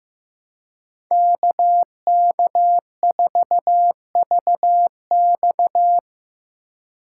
Morse code, heard as a keyed tone, about 18 wpm, 700 Hz. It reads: KK4VX
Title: Speed Racing - USA 2x2 Call Signs 15wpm